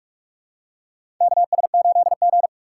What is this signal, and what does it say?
Morse code, heard as a keyed tone, about 48 wpm, 700 Hz. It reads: KS8G
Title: USA Call Signs - 4x Character Spacing - 45wpm